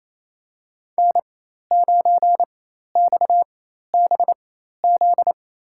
Morse code, heard as a keyed tone, about 28 wpm, spaced as for 13 wpm, 700 Hz. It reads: N9XBZ
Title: USA Call Signs - 5 Characters - 4x Character Spacing - 28wpm